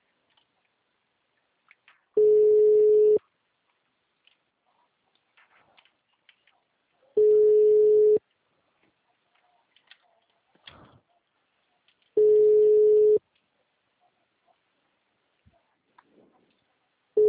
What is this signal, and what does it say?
Ringback tone (Europe, 425 Hz).